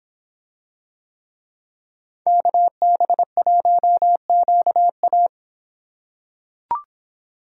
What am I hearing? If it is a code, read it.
Morse code, heard as a keyed tone, about 26 wpm, 700 Hz. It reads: KB1QA